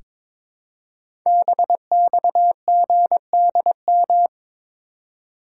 Morse code, heard as a keyed tone, about 22 wpm, 700 Hz. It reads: BXGDM
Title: Code Groups - 5 Letters - 22wpm